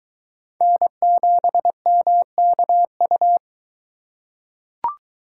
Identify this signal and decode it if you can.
Morse code, heard as a keyed tone, about 21 wpm, 700 Hz. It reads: N7MKU